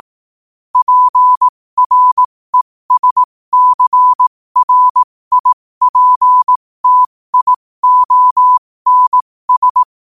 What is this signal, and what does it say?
Morse code, heard as a keyed tone, about 18 wpm, spaced as for 15 wpm, 1000 Hz. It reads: PRESCRIPTIONS
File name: qotdfiles / 15wpm